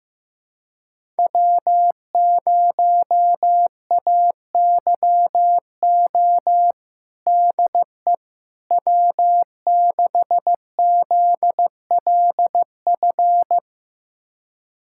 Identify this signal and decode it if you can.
Morse code, heard as a keyed tone, about 15 wpm, 700 Hz. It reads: W0AYO DE W6ZLF
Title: USA Callsign DE Callsign - 5 Characters - 15wpm